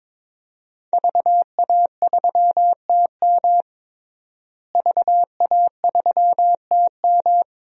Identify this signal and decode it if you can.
Morse code, heard as a keyed tone, about 22 wpm, 700 Hz. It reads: VA3TM VA3TM
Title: DXCC-100 Call Signs - 5 Characters - Repeated 2x - 22wpm